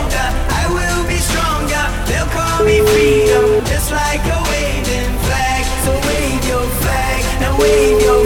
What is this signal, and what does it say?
Ringback tone (Europe, 425 Hz), with noise.